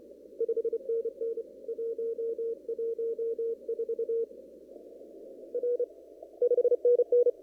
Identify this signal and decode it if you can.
Morse code, heard as a keyed tone, about 28 wpm, 470 Hz. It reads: 5NN 114 R 5NN
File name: CW / kt5j wpx cw-op-k5pi